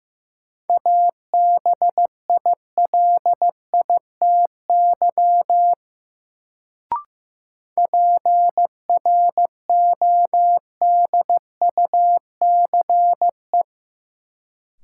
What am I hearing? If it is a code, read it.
Morse code, heard as a keyed tone, about 15 wpm, 700 Hz. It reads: ABILITY PRODUCE
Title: Sentences from Top 1000 Words - Review Part 3 - New Word Drill - 15wpm